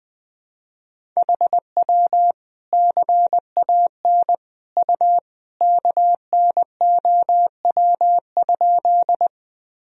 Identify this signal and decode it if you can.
Morse code, heard as a keyed tone, about 20 wpm, 700 Hz. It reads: HW CAN U KNOW?
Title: Sentences from Top 100 Words and Abbreviations 20wpm